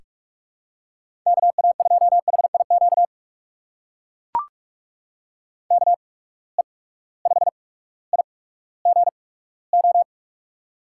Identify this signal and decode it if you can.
Morse code, heard as a keyed tone, about 44 wpm, spaced as for 19 wpm, 700 Hz. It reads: KA2HIQ KE5IGO
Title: USA Call Signs - 6 Characters - 8x Character Spacing - 45wpm